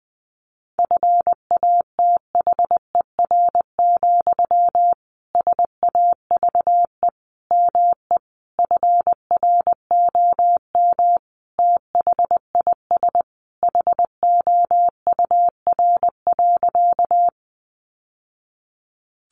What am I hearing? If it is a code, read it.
Morse code, heard as a keyed tone, about 20 wpm, 700 Hz. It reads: FATHER, SAVE ME FROM THIS HOUR.
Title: Sentences from Top 1100 Words 20wpm